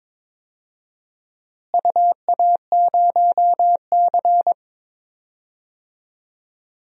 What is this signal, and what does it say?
Morse code, heard as a keyed tone, about 22 wpm, 700 Hz. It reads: UA0C
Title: DXCC-100 Call Signs - 4 Characters - Repeated 2x - 22wpm